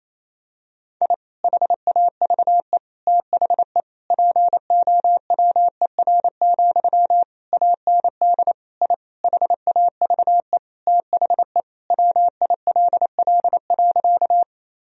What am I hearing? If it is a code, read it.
Morse code, heard as a keyed tone, about 27 wpm, 700 Hz. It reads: I HAVE THE POWER, AND I HAVE THE WILL.
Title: Sentences from Top 1000 Words - Review Part 2 - 28wpm